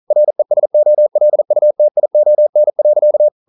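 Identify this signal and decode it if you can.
Morse code, heard as a keyed tone, about 42 wpm, 600 Hz. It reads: RESOLUTION.
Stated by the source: Morse code practice at 41 wpm, from CW med Gnister.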